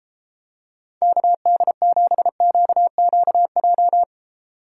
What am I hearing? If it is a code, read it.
Morse code, heard as a keyed tone, about 31 wpm, 700 Hz. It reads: KD7QQJ